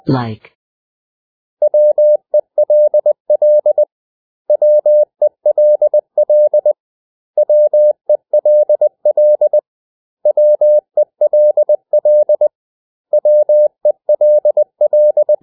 Morse code, heard as a keyed tone, about 20 wpm, 600 Hz. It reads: WELL WELL WELL WELL WELL
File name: cw / 100 Common English Words